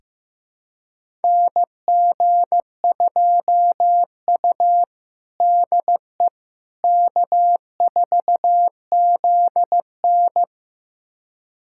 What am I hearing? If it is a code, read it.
Morse code, heard as a keyed tone, about 16 wpm, 700 Hz. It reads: NG2U DE K4ZN
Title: USA Callsign DE Callsign - 4 Characters - 15wpm